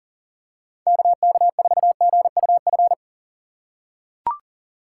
Morse code, heard as a keyed tone, about 40 wpm, 700 Hz. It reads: KK4GUF
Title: USA Call Signs - 3x Character Spacing - 40wpm